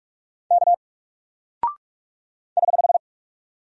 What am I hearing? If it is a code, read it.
Morse code, heard as a keyed tone, about 44 wpm, 700 Hz. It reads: K <HH>
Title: Single Character (Letter:Number:Punctuation:Prosign) 45wpm